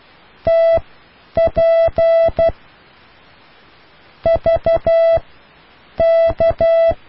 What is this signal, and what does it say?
Morse code, heard as a keyed tone, about 12 wpm, spaced as for 9 wpm, 660 Hz. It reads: TP V K